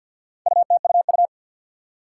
Morse code, heard as a keyed tone, about 53 wpm, 700 Hz. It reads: UTUU
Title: Four Challenging Letter-Number Combinations 50wpm